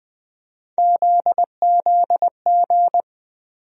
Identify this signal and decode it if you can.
Morse code, heard as a keyed tone, about 21 wpm, 700 Hz. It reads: ZZG
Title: USA Call Sign Postfixes - 3 Letters 20wpm